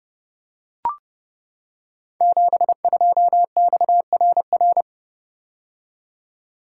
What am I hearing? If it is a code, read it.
Morse code, heard as a keyed tone, about 30 wpm, 700 Hz. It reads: 72XRR
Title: Code Groups - 5 Letter:Numbers - 30wpm